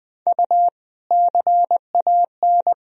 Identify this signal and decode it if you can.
Morse code, heard as a keyed tone, about 21 wpm, 700 Hz. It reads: U CAN
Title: Sentences from Top 100 Words and Abbreviations 20wpm